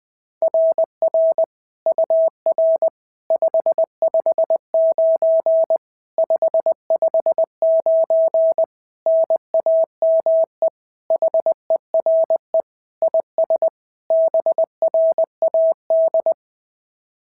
Morse code, heard as a keyed tone, about 21 wpm, 690 Hz. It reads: RR UR 559 559 NAME HERE IS BRAD
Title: SOTA QSO Practice 20wpm